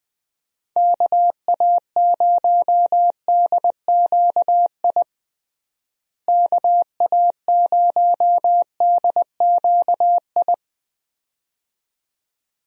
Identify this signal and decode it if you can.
Morse code, heard as a keyed tone, about 20 wpm, 700 Hz. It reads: KA0DQI KA0DQI
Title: USA Call Signs - 6 Characters - Repeated 2x - 20wpm